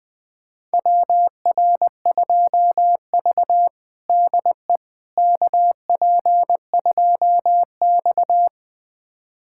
Morse code, heard as a keyed tone, about 20 wpm, 700 Hz. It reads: WR2V DE KP2X
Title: USA Callsign DE Callsign - 4 Characters - 20wpm